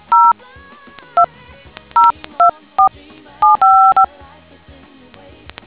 DTMF, heard as telephone touch tones, tones of about 150 ms. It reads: 0202475